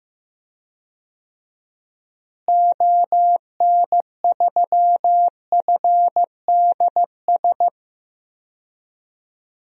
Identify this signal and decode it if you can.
Morse code, heard as a keyed tone, about 15 wpm, 700 Hz. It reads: ON3FDS
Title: DXCC-100 Call Signs - 6 Characters - Repeated 2x - 15wpm